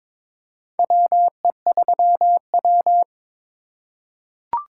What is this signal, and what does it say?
Morse code, heard as a keyed tone, about 22 wpm, 700 Hz. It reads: WE3W
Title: USA Call Signs - 2x Character Spacing - 22wpm